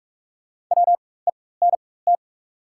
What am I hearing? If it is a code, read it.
Morse code, heard as a keyed tone, about 44 wpm, spaced as for 21 wpm, 700 Hz. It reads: WENT
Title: 4x Character Spacing - Top 200 Words - 45wpm